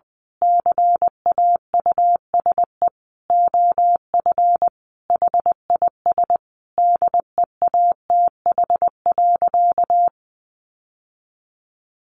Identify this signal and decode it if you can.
Morse code, heard as a keyed tone, about 20 wpm, 700 Hz. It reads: CAUSE OF HIS DEATH.
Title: Sentences from Top 1100 Words 20wpm